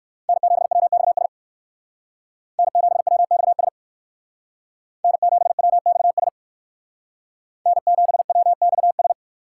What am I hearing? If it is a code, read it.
Morse code, heard as a keyed tone, about 58 wpm, 700 Hz. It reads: N7WXS N7WXS N7WXS N7WXS